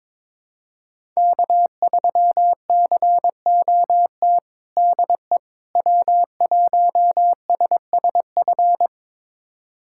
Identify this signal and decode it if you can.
Morse code, heard as a keyed tone, about 21 wpm, 700 Hz. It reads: K3COT DE W1SSF
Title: USA Callsign DE Callsign 22wpm